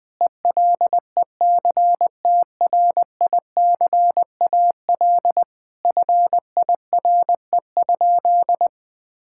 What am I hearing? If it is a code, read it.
Morse code, heard as a keyed tone, about 19 wpm, 700 Hz. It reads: ELECTRICAL FIRE?
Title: Sentences from Top 1000 Words - Review Part 6 - 20wpm